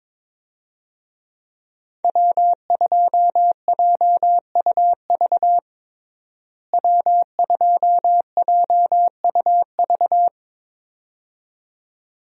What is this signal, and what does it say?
Morse code, heard as a keyed tone, about 22 wpm, 700 Hz. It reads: W2JUV W2JUV
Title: USA Call Signs - 5 Characters - Repeated 2x - 22wpm